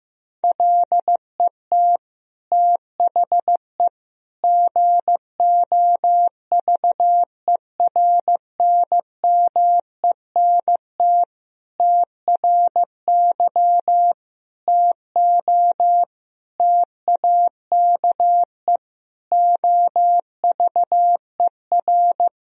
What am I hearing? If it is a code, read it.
Morse code, heard as a keyed tone, about 14 wpm, 700 Hz. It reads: LET THE GOVERNMENT TRY TO TAKE OVER
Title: Sentences from Top 1000 Words - Review Part 3 - 15wpm